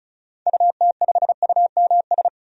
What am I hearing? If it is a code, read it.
Morse code, heard as a keyed tone, about 34 wpm, 700 Hz. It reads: UT5UMS
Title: DXCC-100 Call Signs - 6 Characters - 35wpm